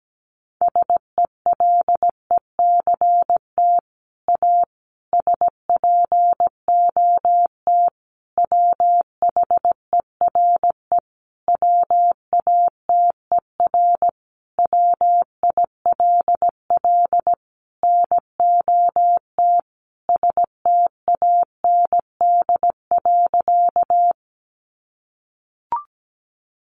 Morse code, heard as a keyed tone, about 17 wpm, 700 Hz. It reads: SELECT A SPOT WHERE WATER WILL NOT STAND.
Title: Sentences from Top 1300 Words 17wpm